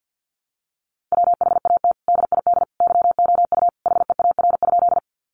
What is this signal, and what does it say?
Morse code, heard as a keyed tone, about 50 wpm, 700 Hz. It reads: WHAT DID YOU HEAR?
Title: Sentences from Top 300 Words 50wpm